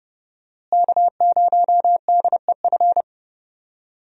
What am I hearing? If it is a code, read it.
Morse code, heard as a keyed tone, about 30 wpm, 700 Hz. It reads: K0DEF
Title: USA Call Signs - 5 Characters - Repeated 2x - 30wpm